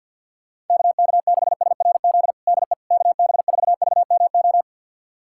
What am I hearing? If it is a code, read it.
Morse code, heard as a keyed tone, about 46 wpm, 700 Hz. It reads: KK6SRZ DE KB4VMO
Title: USA Callsign DE Callsign - 6 Characters - 50wpm